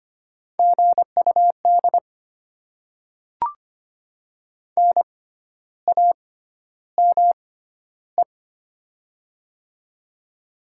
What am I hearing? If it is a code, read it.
Morse code, heard as a keyed tone, about 25 wpm, spaced as for 12 wpm, 700 Hz. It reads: GUD NAME
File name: Top 20 QSO Elements - 6x Character Spacing - 25wpm